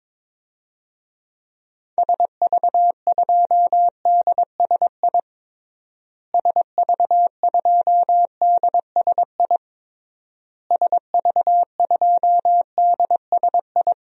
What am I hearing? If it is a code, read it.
Morse code, heard as a keyed tone, about 21 wpm, 700 Hz. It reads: SV2DSI SV2DSI SV2DSI
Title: DXCC-100 Call Signs - 6 Characters - Repeated 3x - 22wpm